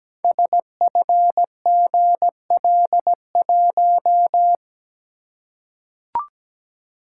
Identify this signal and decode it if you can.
Morse code, heard as a keyed tone, about 17 wpm, 700 Hz. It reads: SFGL1
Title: Code Groups - 5 Letter:Numbers - 17wpm